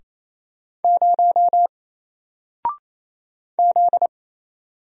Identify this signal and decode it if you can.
Morse code, heard as a keyed tone, about 27 wpm, 700 Hz. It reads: 0 Z
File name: Single-Letter-Number-28wpm